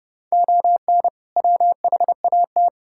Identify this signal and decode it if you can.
Morse code, heard as a keyed tone, about 30 wpm, 700 Hz. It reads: ON WHAT
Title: Sentences from Top 1000 Words - Review Part 5 - 30wpm